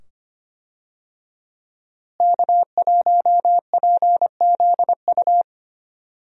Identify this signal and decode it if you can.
Morse code, heard as a keyed tone, about 25 wpm, 700 Hz. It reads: K1PZU
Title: USA Call Signs - 5 Characters - Repeated 2x - 25wpm